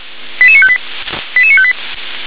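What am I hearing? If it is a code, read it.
Five-tone selective call (ZVEI), heard as quick runs of five tones, tones of about 80 ms. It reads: 88958 88958
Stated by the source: ZVEI five-tone selective call received on SDR.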